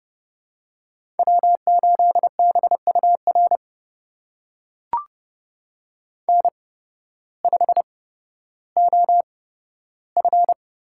Morse code, heard as a keyed tone, about 30 wpm, spaced as for 14 wpm, 700 Hz. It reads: W8BUR N5OF
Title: USA Call Signs - 5 Characters - 8x Character Spacing - 30wpm